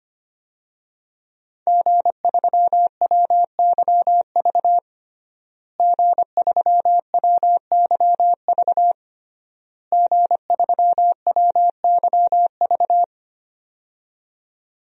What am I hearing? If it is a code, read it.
Morse code, heard as a keyed tone, about 25 wpm, 700 Hz. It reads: G3WYV G3WYV G3WYV
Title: DXCC-100 Call Signs - 5 Characters - Repeated 3x - 25wpm